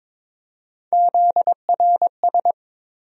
Morse code, heard as a keyed tone, about 22 wpm, 700 Hz. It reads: ZRS
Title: USA Call Sign Postfixes - 3 Letters 22wpm